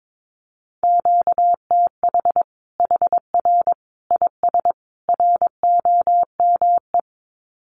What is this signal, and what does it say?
Morse code, heard as a keyed tone, about 22 wpm, 700 Hz. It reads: QTH HR IS ROME
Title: Common QSO Phrases 22wpm